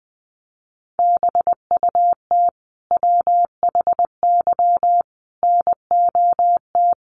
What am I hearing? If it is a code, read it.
Morse code, heard as a keyed tone, about 19 wpm, 700 Hz. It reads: BUT WHY NOT
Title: Sentences from Top 1400 Words 20wpm